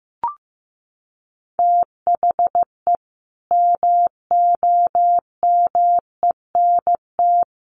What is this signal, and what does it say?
Morse code, heard as a keyed tone, about 15 wpm, 700 Hz. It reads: THE MOMENT
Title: Sentences from Top 1100 Words 15wpm